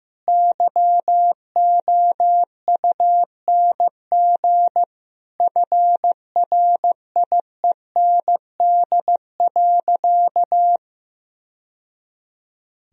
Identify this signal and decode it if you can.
Morse code, heard as a keyed tone, about 16 wpm, 700 Hz. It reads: YOUNG FRIEND.
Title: Sentences from Top 1000 Words - Review Part 3 - 15wpm